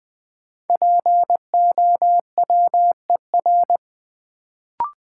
Morse code, heard as a keyed tone, about 21 wpm, 700 Hz. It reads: POWER
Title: Sets of 4 Words - New Word Drill - 20wpm